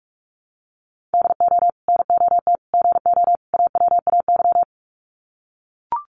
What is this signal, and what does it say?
Morse code, heard as a keyed tone, about 44 wpm, 700 Hz. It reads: DO NOT GO AWAY
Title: Sets of 4 Words 45wpm